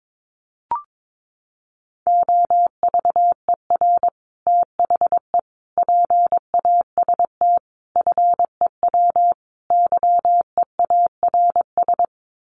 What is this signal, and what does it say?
Morse code, heard as a keyed tone, about 22 wpm, 700 Hz. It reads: OVER THE PAST FEW YEARS
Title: Sets of 5 Words Encore 22wpm